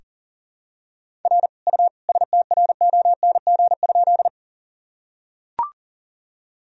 Morse code, heard as a keyed tone, about 40 wpm, 700 Hz. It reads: R U STRONG?
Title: Sentences from Top 500 Words and Abbreviations 40wpm